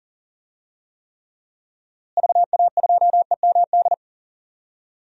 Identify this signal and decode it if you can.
Morse code, heard as a keyed tone, about 40 wpm, 700 Hz. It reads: VA2EMD
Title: DXCC-100 Call Signs - 6 Characters - Repeated 3x - 40wpm